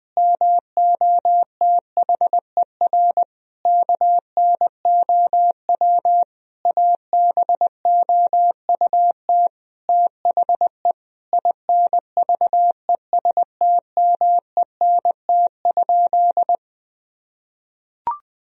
Morse code, heard as a keyed tone, about 20 wpm, 700 Hz. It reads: MOTHER KNOW ABOUT THE INVESTMENT?
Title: Sentences from Top 1000 Words - Review Part 4 - 20wpm